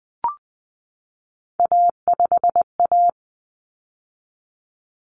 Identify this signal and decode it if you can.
Morse code, heard as a keyed tone, about 20 wpm, 700 Hz. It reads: A5A
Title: Common Cut Numbers - 3-Digit 20wpm